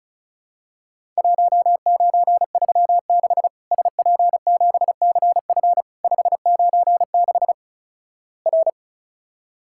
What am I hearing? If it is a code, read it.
Morse code, heard as a keyed tone, about 36 wpm, 700 Hz. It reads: 1936 SP7CF 596 R
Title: Worked All Europe DX Contest - QTC Exchange - 35wpm